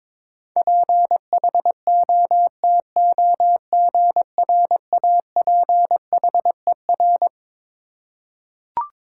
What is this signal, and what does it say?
Morse code, heard as a keyed tone, about 22 wpm, 700 Hz. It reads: PHOTOGRAPHER
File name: Twelve Letter Words - 22wpm